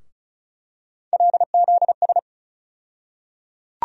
Morse code, heard as a keyed tone, about 36 wpm, 700 Hz. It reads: LZS
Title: USA Call Sign Postfixes - 3 Letters 35wpm